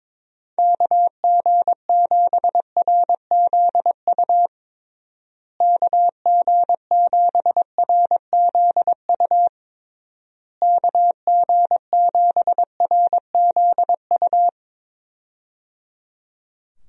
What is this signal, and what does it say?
Morse code, heard as a keyed tone, about 22 wpm, 700 Hz. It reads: KG7RZU KG7RZU KG7RZU
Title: USA Call Signs - 6 Characters - Repeated 3x - 22wpm